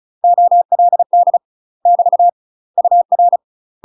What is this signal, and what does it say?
Morse code, heard as a keyed tone, about 36 wpm, 700 Hz. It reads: OLD <BT> UR
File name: UK MASTER QSOs TRAINING/35WPM